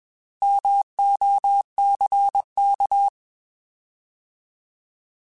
Morse code, heard as a keyed tone, about 21 wpm, 800 Hz. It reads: MOCK
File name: Alphabetical / Alphabetical mp3s / Ms @20